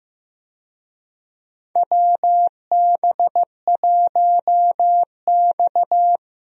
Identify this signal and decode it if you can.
Morse code, heard as a keyed tone, about 15 wpm, 700 Hz. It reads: WB1X
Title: USA Call Signs - 4 Characters - Repeated 2x - 15wpm